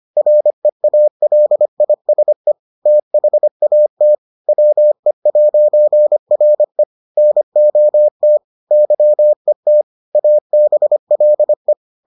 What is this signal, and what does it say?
Morse code, heard as a keyed tone, about 25 wpm, 600 Hz. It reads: REALISE THAT WE'RE NOT YET ABLE